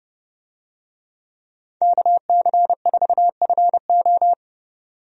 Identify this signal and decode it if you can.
Morse code, heard as a keyed tone, about 29 wpm, 700 Hz. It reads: KC4FO
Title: USA Call Signs - 5 Characters - 30wpm